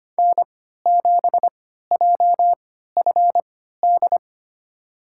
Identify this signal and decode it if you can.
Morse code, heard as a keyed tone, about 25 wpm, spaced as for 14 wpm, 700 Hz. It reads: N7JFD
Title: USA Call Signs - 3x Character Spacing - 25wpm